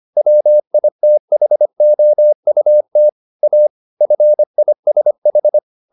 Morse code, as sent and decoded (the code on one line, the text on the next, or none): .-- .. - .... --- ..- - / .- / ..-. .. ... ....
WITHOUT A FISH